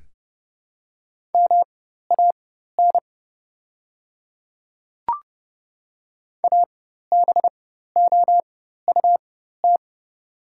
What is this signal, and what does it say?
Morse code, heard as a keyed tone, about 31 wpm, spaced as for 14 wpm, 690 Hz. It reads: MAN ABOUT
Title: 4x Character Spacing - Top 500 Words - 30wpm